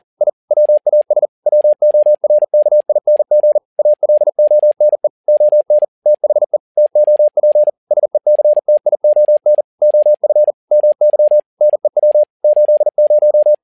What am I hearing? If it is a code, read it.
Morse code, heard as a keyed tone, about 40 wpm, 600 Hz. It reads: I WAS WORKING ALONE ON THE TOP SECTION OF MY NEW 80
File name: learning center / tower trouble / tower touble 40wpm